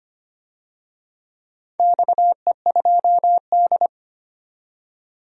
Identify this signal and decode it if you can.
Morse code, heard as a keyed tone, about 25 wpm, 700 Hz. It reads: XE2D
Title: DXCC-100 Call Signs - 4 Characters - 25wpm